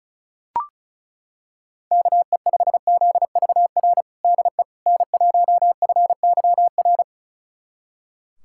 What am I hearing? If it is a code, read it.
Morse code, heard as a keyed tone, about 34 wpm, 700 Hz. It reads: KE5ZVR DE N1FYR
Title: USA Callsign DE Callsign 35wpm